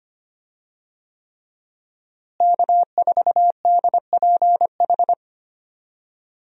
Morse code, heard as a keyed tone, about 25 wpm, 700 Hz. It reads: K4DPH
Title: USA Call Signs - 5 Characters - Repeated 3x - 25wpm